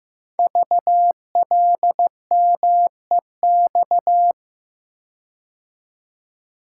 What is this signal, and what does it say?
Morse code, heard as a keyed tone, about 15 wpm, 700 Hz. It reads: VLMEX
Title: Code Groups - 5 Letters - 15wpm